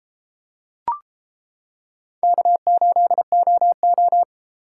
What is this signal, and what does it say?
Morse code, heard as a keyed tone, about 33 wpm, 700 Hz. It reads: K8OO